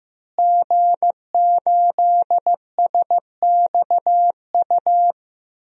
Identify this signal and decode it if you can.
Morse code, heard as a keyed tone, about 15 wpm, 700 Hz. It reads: G8SXU
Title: DXCC-100 Call Signs - 5 Characters - Repeated 2x - 15wpm